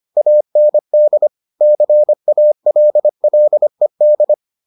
Morse code, heard as a keyed tone, about 25 wpm, 600 Hz. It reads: AND CALLED